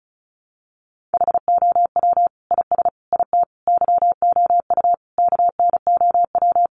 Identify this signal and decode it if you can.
Morse code, heard as a keyed tone, about 34 wpm, 700 Hz. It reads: HOW IS IT YOU KNOW
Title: Sentences from Top 700 Words 35wpm